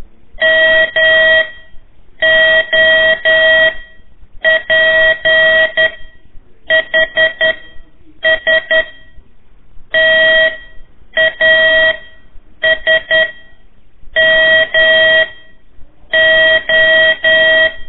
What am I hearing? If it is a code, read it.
Morse code, heard as a keyed tone, about 11 wpm, spaced as for 7 wpm, 1900 Hz. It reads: MOPHS TASMO
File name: 3 lezione CW 1 AIR